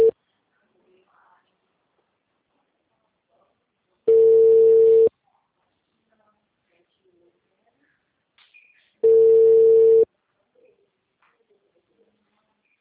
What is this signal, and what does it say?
Telephone call progress tone (China, 450 Hz), heard as a telephone network's tone, about 450 Hz. Ringback tone.